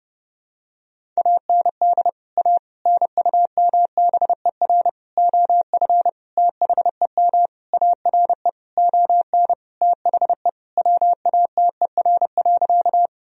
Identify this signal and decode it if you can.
Morse code, heard as a keyed tone, about 32 wpm, 700 Hz. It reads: AND A NUMBER OF THEM ARE ON THE WATER.
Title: Sentences from Top 1000 Words - Review Part 1 - 30wpm